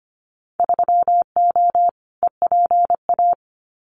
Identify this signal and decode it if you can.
Morse code, heard as a keyed tone, about 25 wpm, 700 Hz. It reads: 3O EPA
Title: Winter Field Day Exchanges 25wpm